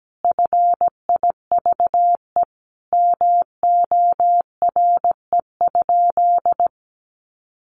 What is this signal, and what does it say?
Morse code, heard as a keyed tone, about 17 wpm, 700 Hz. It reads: FIVE MORE?
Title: Sentences from Top 1400 Words 17wpm